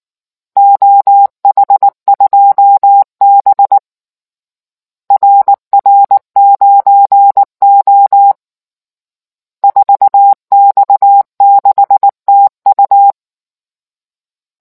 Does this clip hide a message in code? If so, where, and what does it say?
Yes: Morse code from 0.6 to 13.1 s, heard as a keyed tone, about 19 wpm, 800 Hz. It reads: OH2B RR9O 4X6TU